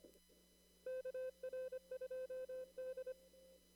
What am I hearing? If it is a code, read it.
Morse code, heard as a keyed tone, about 23 wpm, 520 Hz. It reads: KR2D
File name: CW / kt5j wpx cw-op-k5pi